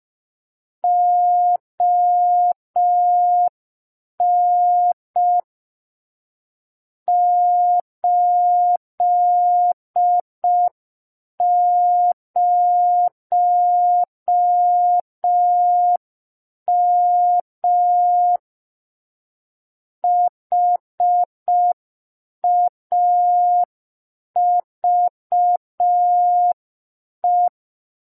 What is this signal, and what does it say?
Morse code, heard as a keyed tone, about 5 wpm, 710 Hz. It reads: ON 80M HAVE